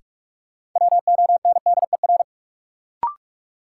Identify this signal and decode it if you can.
Morse code, heard as a keyed tone, about 46 wpm, 700 Hz. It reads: WONDER
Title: Sentences from Top 500 Words - New Word Drill - 45wpm